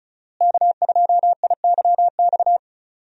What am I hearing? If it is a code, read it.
Morse code, heard as a keyed tone, about 34 wpm, 700 Hz. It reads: K2IYX
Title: USA Call Signs - 5 Characters - Repeated 2x - 35wpm